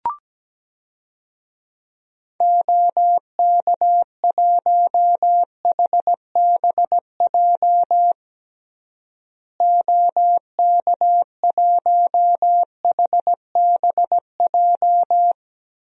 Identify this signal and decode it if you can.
Morse code, heard as a keyed tone, about 17 wpm, 700 Hz. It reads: OK1HBJ OK1HBJ
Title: DXCC-100 Call Signs - 6 Characters - Repeated 3x - 17wpm